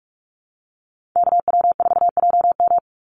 Morse code, heard as a keyed tone, about 46 wpm, 690 Hz. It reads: KW4JM
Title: NA SOTA Chaser Callsigns 45wpm